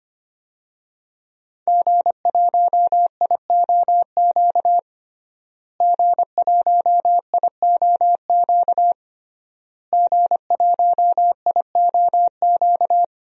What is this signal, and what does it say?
Morse code, heard as a keyed tone, about 25 wpm, 700 Hz. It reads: G1IOQ G1IOQ G1IOQ
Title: DXCC-100 Call Signs - 5 Characters - Repeated 3x - 25wpm